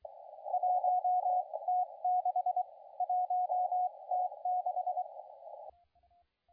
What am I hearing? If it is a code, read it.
Morse code, heard as a keyed tone, about 23 wpm, spaced as for 20 wpm, 730 Hz, with radio noise. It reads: 1T6 1T6
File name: la4c / sac-cw-2017 / CONTEST AUDIO 2100Z